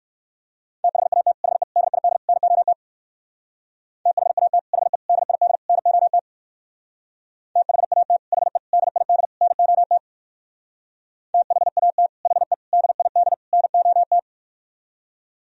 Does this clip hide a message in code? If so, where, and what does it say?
Yes: Morse code from 0.8 to 14.2 s, heard as a keyed tone, about 65 wpm, 700 Hz. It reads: THAT HE DID NOT THAT HE DID NOT THAT HE DID NOT THAT HE DID NOT